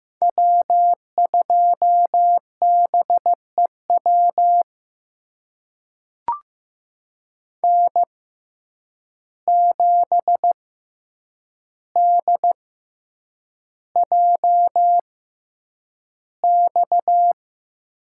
Morse code, heard as a keyed tone, about 15 wpm, spaced as for 8 wpm, 700 Hz. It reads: W2BEW N7DJX
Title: USA Call Signs - 5 Characters - 6x Character Spacing - 15wpm